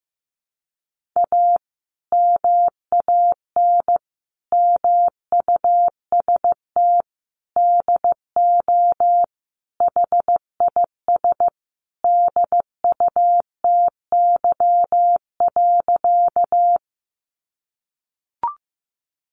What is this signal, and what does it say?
Morse code, heard as a keyed tone, about 15 wpm, 700 Hz. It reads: A MAN MUST DO HIS DUTY.
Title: Sentences from Top 1500 Words 15wpm